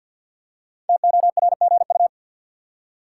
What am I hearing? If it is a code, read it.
Morse code, heard as a keyed tone, about 53 wpm, 700 Hz. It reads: TORGU
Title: Code Groups - 5 Letters - 50wpm